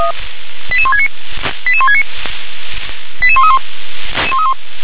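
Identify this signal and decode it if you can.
Five-tone selective call (ZVEI), heard as quick runs of five tones, tones of about 70 ms. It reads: 88169 88169 88131
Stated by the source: Five-tone ZVEI selective call received by SDR.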